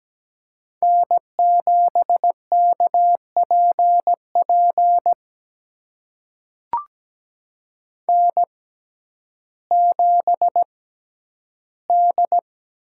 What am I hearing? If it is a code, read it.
Morse code, heard as a keyed tone, about 17 wpm, spaced as for 10 wpm, 700 Hz. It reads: N7KPP N7D
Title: USA Call Signs - 5 Characters - 6x Character Spacing - 17wpm